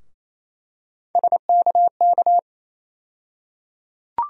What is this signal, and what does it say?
Morse code, heard as a keyed tone, about 28 wpm, 700 Hz. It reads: SKK
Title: USA Call Sign Postfixes - 3 Letters 28wpm